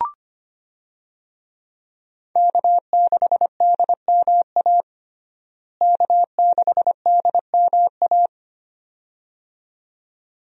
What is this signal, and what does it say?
Morse code, heard as a keyed tone, about 25 wpm, 700 Hz. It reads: K6DMA K6DMA
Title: USA Call Signs - 5 Characters - Repeated 2x - 25wpm